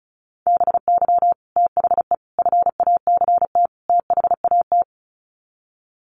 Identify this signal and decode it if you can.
Morse code, heard as a keyed tone, about 34 wpm, 700 Hz. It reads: BY THE FACT THAT
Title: Sets of 4 Words 35wpm